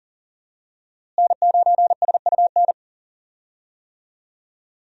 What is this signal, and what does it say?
Morse code, heard as a keyed tone, about 40 wpm, 700 Hz. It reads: N9SUN
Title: USA Call Signs - 5 Characters - Repeated 3x - 40wpm